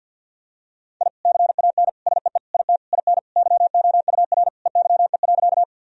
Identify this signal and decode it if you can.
Morse code, heard as a keyed tone, about 52 wpm, 700 Hz. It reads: I CAN SEE IT IN YOUR EYE.
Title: Sentences from Top 1000 Words - Review Part 1 - 50wpm